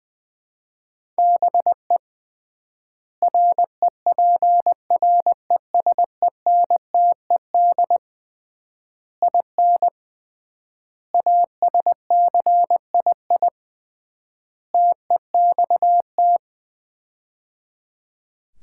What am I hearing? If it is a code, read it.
Morse code, heard as a keyed tone, about 20 wpm, 690 Hz. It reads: BE REPRESENTED IN ASCII TEXT
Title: 3x Word-Spacing - Sets of 5 Words - 20wpm